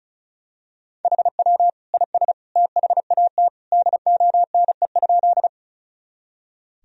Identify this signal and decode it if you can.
Morse code, heard as a keyed tone, about 36 wpm, 700 Hz. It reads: HW IS THAT DONE?
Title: Sentences from Top 500 Words and Abbreviations 35wpm